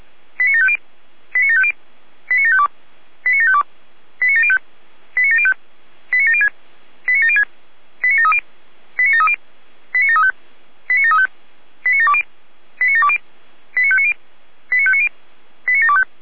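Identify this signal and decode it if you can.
Five-tone selective call (ZVEI), heard as quick runs of five tones, tones of about 70 ms. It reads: 89750 89750 89752 89752 89705 89705 89706 89706 89730 89730 89735 89735 89720 89720 89590 89590 89725
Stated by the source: Received on an SDR as a ZVEI five-tone call.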